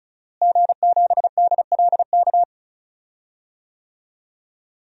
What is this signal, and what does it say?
Morse code, heard as a keyed tone, about 35 wpm, 700 Hz. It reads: G7DLK
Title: DXCC-100 Call Signs - 5 Characters - Repeated 2x - 35wpm